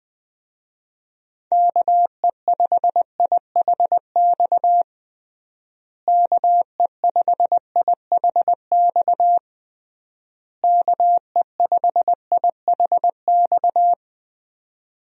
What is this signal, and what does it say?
Morse code, heard as a keyed tone, about 21 wpm, 700 Hz. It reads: KE5IHX KE5IHX KE5IHX
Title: USA Call Signs - 6 Characters - Repeated 3x - 20wpm